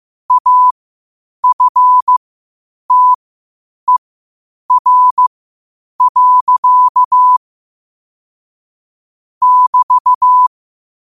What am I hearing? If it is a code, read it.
Morse code, heard as a keyed tone, about 15 wpm, spaced as for 8 wpm, 1000 Hz. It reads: AFTER. <BT>
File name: qotdfiles / 8wpm